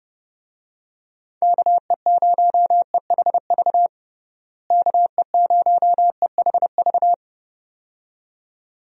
Morse code, heard as a keyed tone, about 32 wpm, 700 Hz. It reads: KE0EHV KE0EHV
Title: USA Call Signs - 6 Characters - Repeated 2x - 30wpm